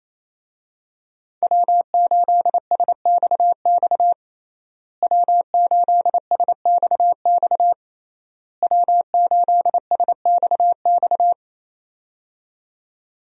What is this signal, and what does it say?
Morse code, heard as a keyed tone, about 27 wpm, 700 Hz. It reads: W8SXX W8SXX W8SXX
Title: USA Call Signs - 5 Characters - Repeated 3x - 28wpm